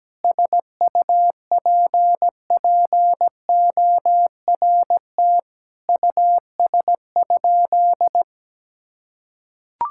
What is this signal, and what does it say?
Morse code, heard as a keyed tone, about 17 wpm, 700 Hz. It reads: SUPPORT US?
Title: Sentences from Top 1000 Words - Review Part 5 - 17wpm